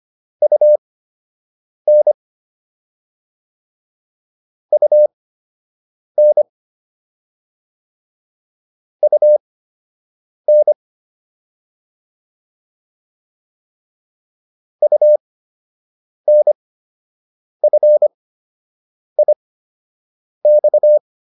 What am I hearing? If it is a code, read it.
Morse code, heard as a keyed tone, about 25 wpm, spaced as for 7 wpm, 600 Hz. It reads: UN UN UN UNFIX